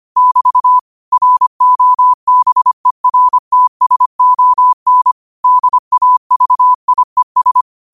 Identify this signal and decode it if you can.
Morse code, heard as a keyed tone, about 25 wpm, 1000 Hz. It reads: <BT> ROBERTSON DAVIES